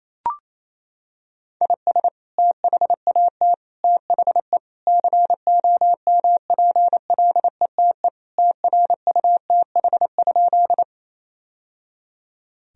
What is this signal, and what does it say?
Morse code, heard as a keyed tone, about 28 wpm, 700 Hz. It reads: IS THAT THE COMPLETE TRUTH?
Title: Sentences from Top 1000 Words - Review Part 2 - 28wpm